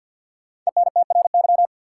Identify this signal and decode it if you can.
Morse code, heard as a keyed tone, about 51 wpm, 700 Hz. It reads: ENTRY
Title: Sentences from Top 1500 Words - New Word Drill - 50wpm